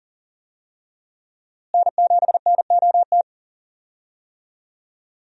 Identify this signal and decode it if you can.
Morse code, heard as a keyed tone, about 40 wpm, 700 Hz. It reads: N7NOT
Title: USA Call Signs - 5 Characters - Repeated 3x - 40wpm